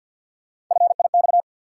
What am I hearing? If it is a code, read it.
Morse code, heard as a keyed tone, about 53 wpm, 700 Hz. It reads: FIX
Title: Sentences from Top 1400 Words - New Word Drill Alt - 50wpm